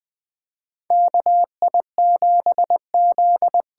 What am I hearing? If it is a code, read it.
Morse code, heard as a keyed tone, about 20 wpm, 700 Hz. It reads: KI7Z
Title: USA Call Signs - 6 Characters - 8x Character Spacing - 20wpm